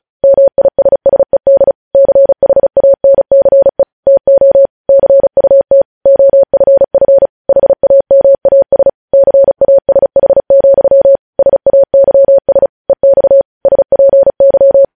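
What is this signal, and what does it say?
Morse code, heard as a keyed tone, about 36 wpm, 550 Hz. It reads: MISSED CHANCE TO CUT OFF HAMAS CASH, SAYS EX SPY
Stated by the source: News headlines sent in Morse at 35 wpm.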